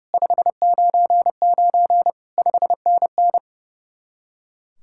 Morse code, heard as a keyed tone, about 35 wpm, 700 Hz. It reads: 599 5NN
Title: Speed-Racing Common QSO Phrases - 30wpm